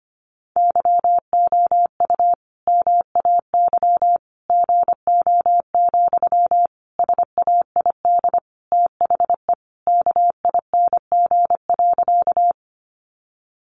Morse code, heard as a keyed tone, about 25 wpm, 700 Hz. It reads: YOU MAY GO, SAID THE KING.
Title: Sentences from Top 400 Words 25wpm